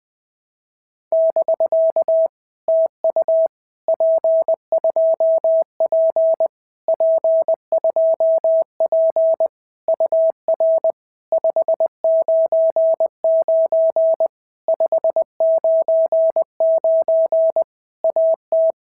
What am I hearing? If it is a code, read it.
Morse code, heard as a keyed tone, about 20 wpm, 650 Hz. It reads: <BK> TU P2P P2P UR 599 599 AT
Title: POTA QSO Practice - 20wpm